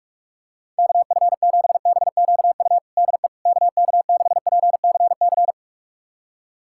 Morse code, heard as a keyed tone, about 46 wpm, 700 Hz. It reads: KF7BQU DE KK6PCC
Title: USA Callsign DE Callsign 45wpm